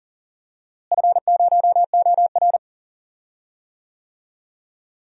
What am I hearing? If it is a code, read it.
Morse code, heard as a keyed tone, about 40 wpm, 700 Hz. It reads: F0OR
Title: DXCC-100 Call Signs - 4 Characters - Repeated 2x - 40wpm